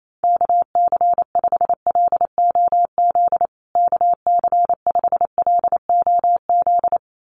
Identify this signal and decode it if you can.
Morse code, heard as a keyed tone, about 28 wpm, 700 Hz. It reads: KC5LOZ KC5LOZ
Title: USA Call Signs Repeated 2x 28wpm